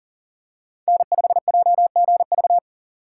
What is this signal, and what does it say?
Morse code, heard as a keyed tone, about 37 wpm, 700 Hz. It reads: N5JGV
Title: USA Call Signs - 2x Character Spacing - 40wpm